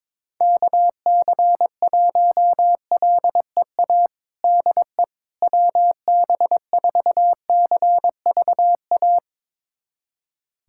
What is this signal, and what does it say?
Morse code, heard as a keyed tone, about 22 wpm, 700 Hz. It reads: KC1LEA DE WB4CVA
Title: USA Callsign DE Callsign 22wpm